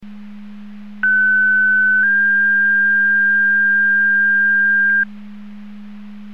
Two-tone page, heard as a one-second tone, then a three-second tone, about 1500 then 1600 Hz. It is heard from 1.0 to 5.0 s.